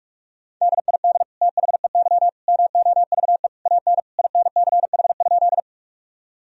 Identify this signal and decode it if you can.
Morse code, heard as a keyed tone, about 46 wpm, 700 Hz. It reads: DID THEY MOVE AN INCH?
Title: Sentences from Top 1000 Words - Review Part 3 - 45wpm